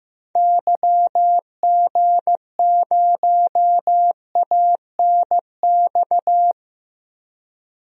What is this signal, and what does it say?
Morse code, heard as a keyed tone, about 15 wpm, 700 Hz. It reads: YG0ANX
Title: DXCC-100 Call Signs - 6 Characters - 15wpm